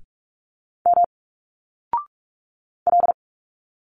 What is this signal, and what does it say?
Morse code, heard as a keyed tone, about 46 wpm, 700 Hz. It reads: M L
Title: Single Letters 45wpm